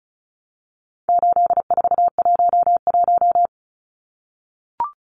Morse code, heard as a keyed tone, about 34 wpm, 700 Hz. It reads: 8411
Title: Numbers - 4-Digit 35wpm